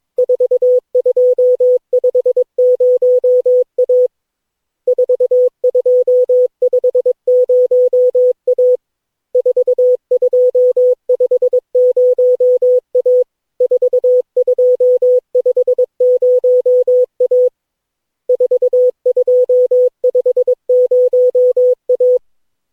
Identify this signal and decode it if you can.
Morse code, heard as a keyed tone, about 22 wpm, 500 Hz. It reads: 4250A 4250A 4250A 4250A 4250A
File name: cw / Session9 cw training